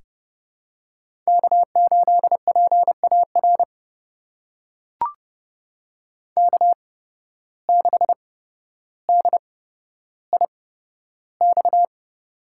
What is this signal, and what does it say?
Morse code, heard as a keyed tone, about 30 wpm, spaced as for 13 wpm, 700 Hz. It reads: K8PAR K6DIX
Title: USA Call Signs - 5 Characters - 8x Character Spacing - 30wpm